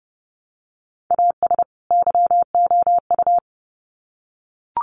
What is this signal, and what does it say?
Morse code, heard as a keyed tone, about 30 wpm, 700 Hz. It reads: AS YOU
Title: Sets of 2 Words 30wpm